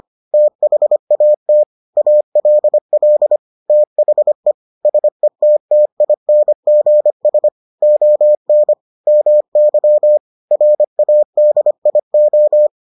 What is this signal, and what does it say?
Morse code, heard as a keyed tone, about 25 wpm, 600 Hz. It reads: THAT ALL THE SETTINGS ON MY RADIO